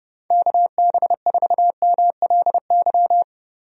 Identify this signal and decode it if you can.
Morse code, heard as a keyed tone, about 32 wpm, 700 Hz. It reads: KB4MLY
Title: USA Call Signs - 6 Characters - Repeated 2x - 30wpm